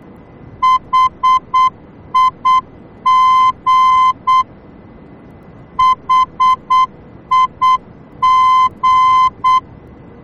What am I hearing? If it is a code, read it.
Morse code, heard as a keyed tone, about 8 wpm, 1030 Hz. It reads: HIG HIG
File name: germany / mx / HIG-D